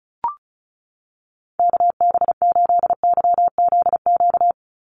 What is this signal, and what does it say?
Morse code, heard as a keyed tone, about 34 wpm, 700 Hz. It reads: KB8YZQ
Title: USA Call Signs 35wpm